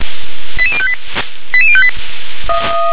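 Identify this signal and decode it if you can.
Five-tone selective call (ZVEI), heard as quick runs of five tones, tones of about 70 ms. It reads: 88058 88058
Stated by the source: ZVEI five-tone selective call received by SDR.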